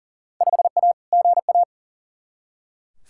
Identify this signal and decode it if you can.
Morse code, heard as a keyed tone, about 40 wpm, 700 Hz. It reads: 5A GA
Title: Field Day Exchanges Encore - 40wpm